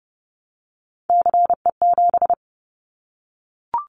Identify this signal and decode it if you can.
Morse code, heard as a keyed tone, about 30 wpm, 700 Hz. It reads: CE7
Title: Callsign Prefixes for Top 100 DXCC - 30wpm